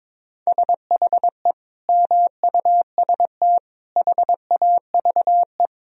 Morse code, heard as a keyed tone, about 22 wpm, 700 Hz. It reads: SHE MUST HAVE
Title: Sentences from Top 1000 Words - Review Part 1 - 22wpm